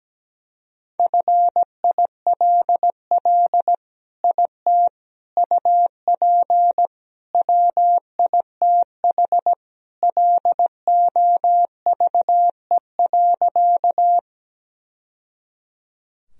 Morse code, heard as a keyed tone, about 17 wpm, 700 Hz. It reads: FILL IT UP WITH LOVE.
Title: Sentences from Top 1000 Words - Review Part 3 - 17wpm